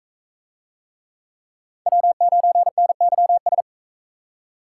Morse code, heard as a keyed tone, about 44 wpm, 700 Hz. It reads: W9NYS